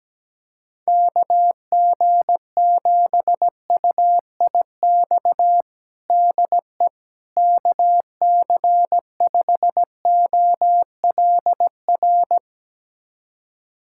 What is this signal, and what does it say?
Morse code, heard as a keyed tone, about 17 wpm, 700 Hz. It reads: KG7UIX DE KC5OLR
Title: USA Callsign DE Callsign - 6 Characters - 17wpm